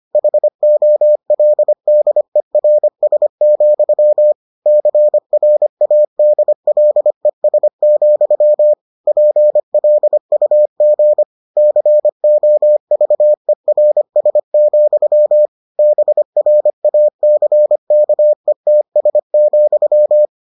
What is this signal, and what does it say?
Morse code, heard as a keyed tone, about 25 wpm, 600 Hz. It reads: HOLDERS, CRADLES, PLUG COVERS, BRACKETS,